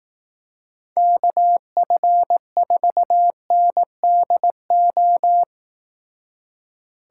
Morse code, heard as a keyed tone, about 17 wpm, 700 Hz. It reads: KF4NDO
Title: Speed Racing - USA 2x3 Call Signs 15wpm